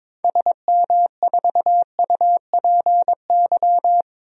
Morse code, heard as a keyed tone, about 22 wpm, 700 Hz. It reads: SM4UPY
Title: DXCC-100 Call Signs - 6 Characters - Repeated 2x - 22wpm